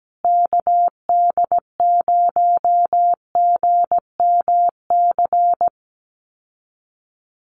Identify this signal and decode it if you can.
Morse code, heard as a keyed tone, about 17 wpm, 700 Hz. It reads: KD0GMC
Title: USA Call Signs Repeated 3x 17wpm